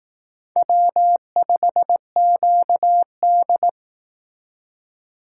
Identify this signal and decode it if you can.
Morse code, heard as a keyed tone, about 18 wpm, 700 Hz. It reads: W5QD
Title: Speed Racing - USA Call Signs 15wpm